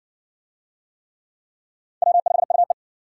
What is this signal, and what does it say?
Morse code, heard as a keyed tone, about 60 wpm, 700 Hz. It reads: W5UE